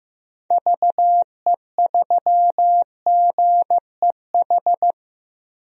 Morse code, heard as a keyed tone, about 15 wpm, 700 Hz. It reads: VE3GEH
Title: DXCC-100 Call Signs - 6 Characters - Repeated 3x - 15wpm